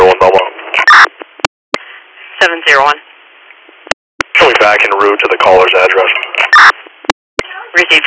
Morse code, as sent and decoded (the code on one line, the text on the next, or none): none